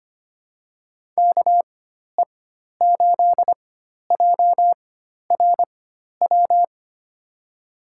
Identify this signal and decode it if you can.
Morse code, heard as a keyed tone, about 25 wpm, spaced as for 12 wpm, 700 Hz. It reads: KE8JRW
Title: USA Call Signs - 6 Characters - 4x Character Spacing - 25wpm